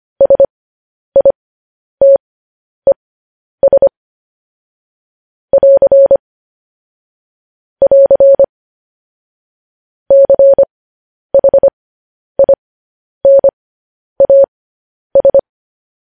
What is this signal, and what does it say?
Morse code, heard as a keyed tone, about 25 wpm, spaced as for 10 wpm, 550 Hz. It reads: SITES <AR> <AR> CHINAS
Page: News Headlines in Morse Code at 10 WPM